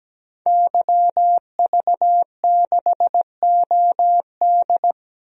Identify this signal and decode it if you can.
Morse code, heard as a keyed tone, about 17 wpm, 700 Hz. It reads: YV6OD
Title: DXCC-100 Call Signs - 5 Characters - 17wpm